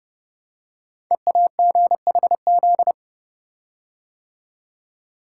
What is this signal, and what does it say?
Morse code, heard as a keyed tone, about 30 wpm, 700 Hz. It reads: EAGHZ
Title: Code Groups - 5 Letters - 30wpm